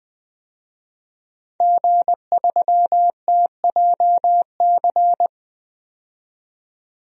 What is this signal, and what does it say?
Morse code, heard as a keyed tone, about 20 wpm, 700 Hz. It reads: G3TJC
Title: DXCC-100 Call Signs - 5 Characters - 20wpm